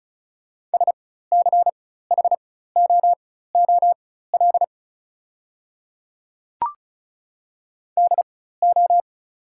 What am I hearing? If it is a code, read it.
Morse code, heard as a keyed tone, about 36 wpm, spaced as for 16 wpm, 690 Hz. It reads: SCHOOL DO
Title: 4x Character Spacing - Top 300 Words - 35wpm